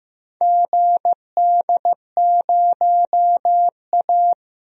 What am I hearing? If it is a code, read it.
Morse code, heard as a keyed tone, about 15 wpm, 700 Hz. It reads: GD0A
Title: DXCC-100 Call Signs - 4 Characters - Repeated 2x - 15wpm